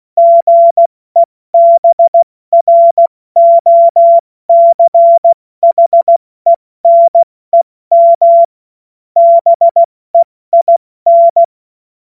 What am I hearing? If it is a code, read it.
Morse code, heard as a keyed tone, about 16 wpm, spaced as for 14 wpm, 680 Hz. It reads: GEBROCHENEM BEIN
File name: Morsekurs / Witze / Witz 03. August 2021, 70 F80